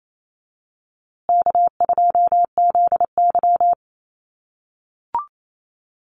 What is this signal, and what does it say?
Morse code, heard as a keyed tone, about 28 wpm, 700 Hz. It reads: K2ZY
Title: USA Call Signs Repeated 2x 28wpm